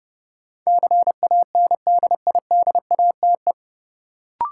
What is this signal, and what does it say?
Morse code, heard as a keyed tone, about 30 wpm, 700 Hz. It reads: CANDIDATE
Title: Sentences from Top 1600 Words - New Word Drill Alt - 30wpm